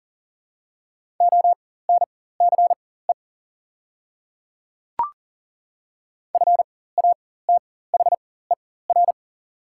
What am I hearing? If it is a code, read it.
Morse code, heard as a keyed tone, about 39 wpm, spaced as for 18 wpm, 700 Hz. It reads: ONCE FATHER
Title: 4x Character Spacing - Top 300 Words - 40wpm